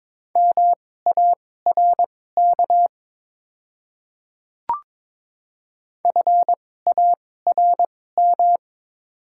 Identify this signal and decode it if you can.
Morse code, heard as a keyed tone, about 22 wpm, spaced as for 16 wpm, 700 Hz. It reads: MARK FARM
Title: 2x Character Spacing - Top 500 Words - 22wpm